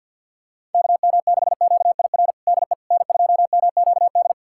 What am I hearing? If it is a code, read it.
Morse code, heard as a keyed tone, about 47 wpm, 700 Hz. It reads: KM6QIR DE N2MQD
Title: USA Callsign DE Callsign 50wpm